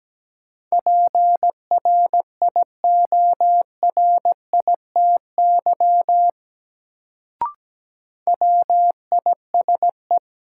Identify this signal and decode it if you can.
Morse code, heard as a keyed tone, about 17 wpm, 700 Hz. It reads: PRIORITY WISE
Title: Sentences from Top 1600 Words - New Word Drill Alt - 17wpm